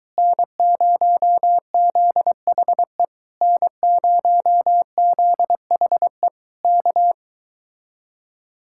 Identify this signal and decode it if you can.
Morse code, heard as a keyed tone, about 24 wpm, 700 Hz. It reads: N0ZHE N0ZHE K